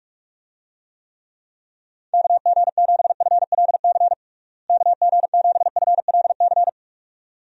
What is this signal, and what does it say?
Morse code, heard as a keyed tone, about 46 wpm, 700 Hz. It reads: KG7FLC KG7FLC
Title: USA Call Signs - 6 Characters - Repeated 2x - 45wpm